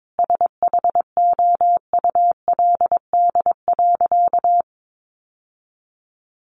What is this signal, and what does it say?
Morse code, heard as a keyed tone, about 22 wpm, 700 Hz. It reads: SHOULD.
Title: Sentences from Top 700 Words 22wpm